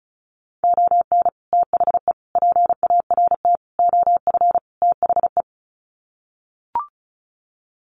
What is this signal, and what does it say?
Morse code, heard as a keyed tone, about 34 wpm, 700 Hz. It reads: ON THE PART OF THE
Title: Sets of 5 Words 35wpm